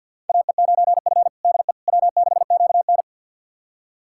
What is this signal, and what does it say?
Morse code, heard as a keyed tone, about 46 wpm, 700 Hz. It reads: AE9F DE W6QN
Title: USA Callsign DE Callsign - 4 Characters - 50wpm